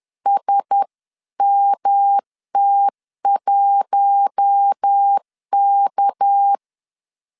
Morse code, heard as a keyed tone, about 10 wpm, 800 Hz. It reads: SMT1K
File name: CW / DET / 1 6 1500C S B 10wpm 4137